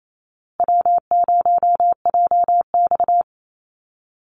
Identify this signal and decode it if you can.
Morse code, heard as a keyed tone, about 28 wpm, 700 Hz. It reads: W0JX
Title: Call Signs Repeated 2x - 28wpm